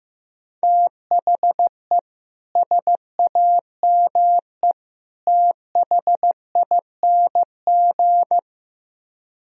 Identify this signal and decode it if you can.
Morse code, heard as a keyed tone, about 17 wpm, 700 Hz. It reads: THE SAME THING
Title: Speed-Racing - Sets of 3 Words 15wpm